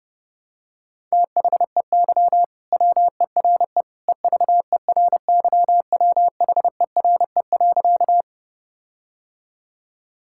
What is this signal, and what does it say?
Morse code, heard as a keyed tone, about 30 wpm, 700 Hz. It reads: THEY WERE EVERYWHERE.
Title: Sentences from Top 1600 Words 30wpm